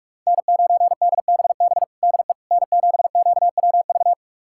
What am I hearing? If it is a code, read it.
Morse code, heard as a keyed tone, about 45 wpm, 700 Hz. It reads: N9DBB DE N7QWV
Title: USA Callsign DE Callsign - 5 Characters - 45wpm